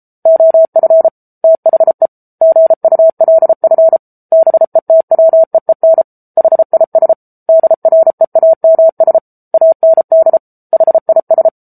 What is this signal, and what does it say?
Morse code, heard as a keyed tone, about 34 wpm, 650 Hz. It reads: OF THE GULF BETWEEN HIS DREAMS AND HIS
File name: Book Two / 07-Chapter 7 - 33 WPM